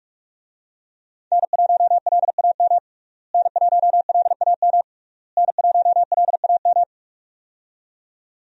Morse code, heard as a keyed tone, about 46 wpm, 700 Hz. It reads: N1LAM N1LAM N1LAM
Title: USA Call Signs - 5 Characters - Repeated 3x - 45wpm